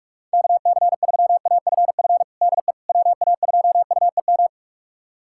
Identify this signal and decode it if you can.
Morse code, heard as a keyed tone, about 44 wpm, 700 Hz. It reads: KC3AFF DE WA2UEM
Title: USA Callsign DE Callsign - 6 Characters - 45wpm